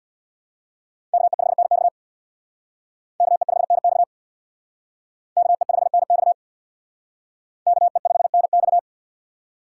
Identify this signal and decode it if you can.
Morse code, heard as a keyed tone, about 62 wpm, 700 Hz. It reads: KE5NX KE5NX KE5NX KE5NX
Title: Speed Racing - USA 2x2 Call Signs 50wpm